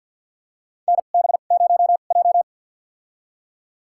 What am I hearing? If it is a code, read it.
Morse code, heard as a keyed tone, about 52 wpm, spaced as for 36 wpm, 700 Hz. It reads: NB0J
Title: USA Call Signs - 2x Character Spacing - 50wpm